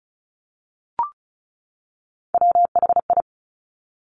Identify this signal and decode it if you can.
Morse code, heard as a keyed tone, about 36 wpm, 700 Hz. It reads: WHI
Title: Top Three Letter Combinations in Words 35wpm